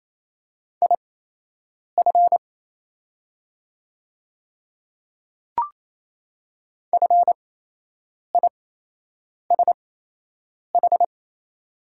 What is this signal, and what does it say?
Morse code, heard as a keyed tone, about 28 wpm, spaced as for 8 wpm, 700 Hz. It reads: IF FISH
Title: 8x Character Spacing - Top 300 Words - 28wpm